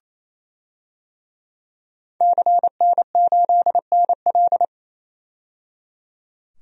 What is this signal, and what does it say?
Morse code, heard as a keyed tone, about 28 wpm, 700 Hz. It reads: CN8NL
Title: DXCC-100 Call Signs - 5 Characters - 28wpm